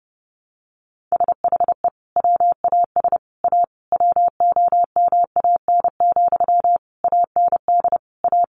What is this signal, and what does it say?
Morse code, heard as a keyed tone, about 30 wpm, 700 Hz. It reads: SHE WAS A WOMAN, AND A
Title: Sentences from Top 1400 Words 30wpm